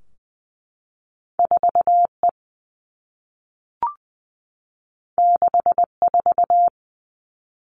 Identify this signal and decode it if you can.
Morse code, heard as a keyed tone, about 19 wpm, 700 Hz. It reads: 4 E 6 4
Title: Two Challenging Letter-Number Combinations 20wpm